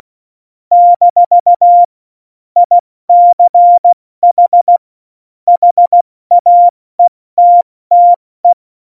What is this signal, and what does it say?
Morse code, heard as a keyed tone, about 16 wpm, spaced as for 14 wpm, 700 Hz. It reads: - ICH HAETTE
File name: Morsekurs / Witze / Witz 23. April 2019, 70 F80